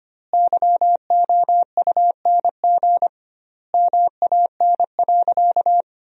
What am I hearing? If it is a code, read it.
Morse code, heard as a keyed tone, about 25 wpm, 700 Hz. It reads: YOUNG MAN.
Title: 2x Word-Spacing - Sentences from Top 400 Words - 25wpm